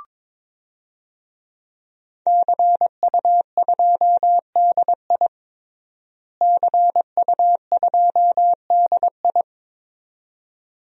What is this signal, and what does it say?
Morse code, heard as a keyed tone, about 21 wpm, 700 Hz. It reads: CU2DI CU2DI
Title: DXCC-100 Call Signs - 5 Characters - Repeated 2x - 22wpm